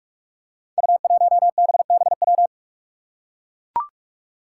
Morse code, heard as a keyed tone, about 44 wpm, 700 Hz. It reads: U1BBW
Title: Five Challenging Letter-Number Combinations 45wpm